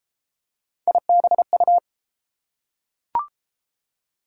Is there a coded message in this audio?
Yes: Morse code from 0.9 to 1.8 s, heard as a keyed tone, about 29 wpm, 700 Hz. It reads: IBU